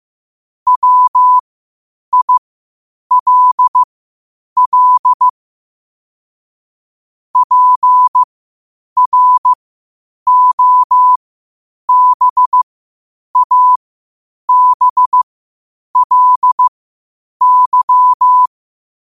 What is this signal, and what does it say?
Morse code, heard as a keyed tone, about 15 wpm, spaced as for 8 wpm, 1000 Hz. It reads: WILL PROBABLY
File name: qotdfiles / 8wpm